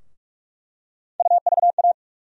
Morse code, heard as a keyed tone, about 46 wpm, 700 Hz. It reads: UVA
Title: USA Call Sign Postfixes - 3 Letters 45wpm